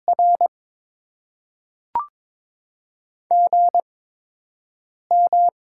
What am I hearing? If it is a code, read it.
Morse code, heard as a keyed tone, about 22 wpm, spaced as for 6 wpm, 700 Hz. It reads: R GM
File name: Top 20 QSO Elements - 8x Character Spacing - 22wpm